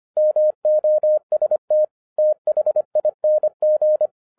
Morse code, heard as a keyed tone, about 25 wpm, 610 Hz. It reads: MOST THING
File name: 2011-10-mp3 / n04-JUST MOVED 25wpm